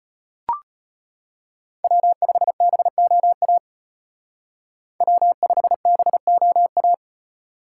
Morse code, heard as a keyed tone, about 35 wpm, 700 Hz. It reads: W5BOA W5BOA